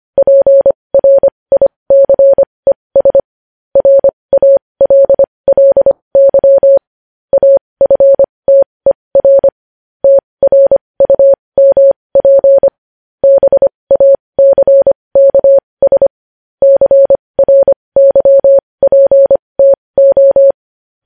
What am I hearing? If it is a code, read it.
Morse code, heard as a keyed tone, about 25 wpm, spaced as for 20 wpm, 550 Hz. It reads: PRICES RALLY AFTER TRUMP BACKS CRYPTO
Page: news-headlines-in-morse-code-at-20-wpm